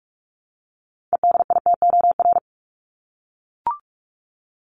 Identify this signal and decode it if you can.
Morse code, heard as a keyed tone, about 44 wpm, 700 Hz. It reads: EDITOR
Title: Top 1501-1750 Words 45wpm